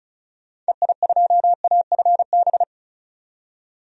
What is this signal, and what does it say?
Morse code, heard as a keyed tone, about 36 wpm, 700 Hz. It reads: EI2AFB
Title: DXCC-100 Call Signs - 6 Characters - 35wpm